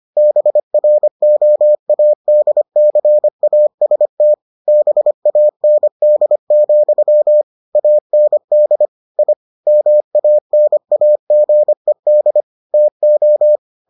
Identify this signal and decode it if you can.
Morse code, heard as a keyed tone, about 25 wpm, 600 Hz. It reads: BROADCAST BAND, AND I MANAGED TO